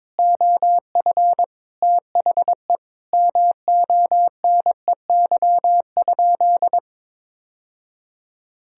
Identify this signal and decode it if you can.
Morse code, heard as a keyed tone, about 22 wpm, 700 Hz. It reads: OF THE MONEY?
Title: Sentences from Top 1000 Words - Review Part 6 - 22wpm